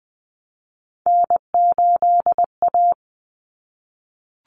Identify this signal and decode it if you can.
Morse code, heard as a keyed tone, about 20 wpm, 700 Hz. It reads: N8A
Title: Common Cut Numbers - 3-Digit 20wpm